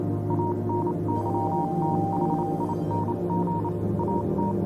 Morse code, heard as a keyed tone, about 28 wpm, spaced as for 25 wpm, 940 Hz, with noise. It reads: ANYTHING AT
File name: Vylet Pony / aura / do you remember the song she sang (morse code version)